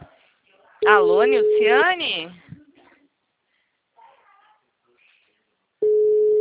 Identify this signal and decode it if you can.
Ringback tone (Europe, 425 Hz), with speech.